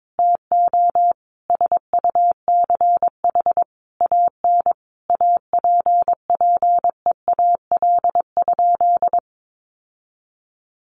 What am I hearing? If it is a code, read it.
Morse code, heard as a keyed tone, about 22 wpm, 700 Hz. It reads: TO SUCH AN APPEAL?
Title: Sentences from Top 1300 Words 22wpm